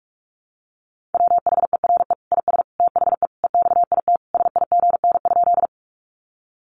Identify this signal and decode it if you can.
Morse code, heard as a keyed tone, about 44 wpm, 700 Hz. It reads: WHERE IS THE EXIT SIGN?
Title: Sentences from Top 1100 Words 45wpm